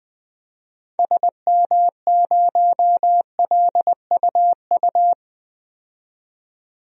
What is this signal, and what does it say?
Morse code, heard as a keyed tone, about 21 wpm, 700 Hz. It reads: SM0LUU
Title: DXCC-100 Call Signs - 6 Characters - 20wpm